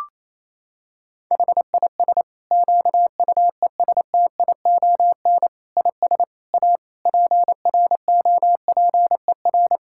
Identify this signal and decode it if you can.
Morse code, heard as a keyed tone, about 27 wpm, 700 Hz. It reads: HIS QUESTION IS A PROPER
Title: Sentences from Top 1000 Words - Review Part 5 - 28wpm